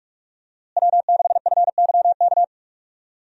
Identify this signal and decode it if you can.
Morse code, heard as a keyed tone, about 45 wpm, 700 Hz. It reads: W6FYK
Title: USA Call Signs - 5 Characters - Repeated 2x - 45wpm